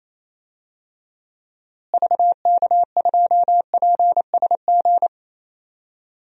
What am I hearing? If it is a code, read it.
Morse code, heard as a keyed tone, about 28 wpm, 700 Hz. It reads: VK2PSG
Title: DXCC-100 Call Signs - 6 Characters - Repeated 2x - 28wpm